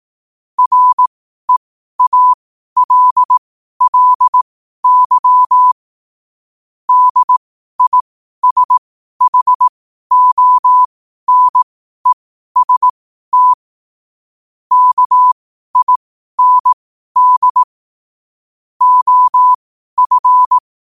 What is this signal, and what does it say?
Morse code, heard as a keyed tone, about 18 wpm, spaced as for 12 wpm, 1000 Hz. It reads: REALLY DISHONEST KIND OF
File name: qotdfiles / 12wpm